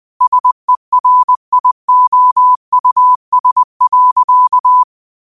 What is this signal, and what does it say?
Morse code, heard as a keyed tone, about 20 wpm, 1000 Hz. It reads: SERIOUS.
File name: qotdfiles / 20wpm